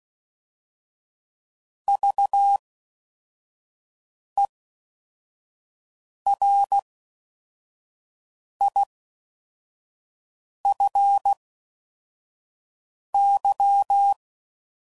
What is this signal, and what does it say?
Morse code, heard as a keyed tone, about 16 wpm, spaced as for 4 wpm, 800 Hz. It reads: VERIFY